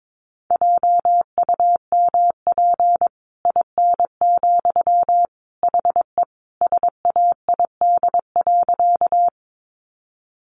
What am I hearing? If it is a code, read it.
Morse code, heard as a keyed tone, about 22 wpm, 700 Hz. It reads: JUMP IN, HE SAID.
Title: Sentences from Top 1300 Words 22wpm